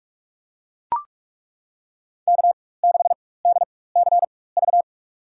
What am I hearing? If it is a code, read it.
Morse code, heard as a keyed tone, about 42 wpm, spaced as for 20 wpm, 700 Hz. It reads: K6DCV
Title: USA Call Signs - 4x Character Spacing - 45wpm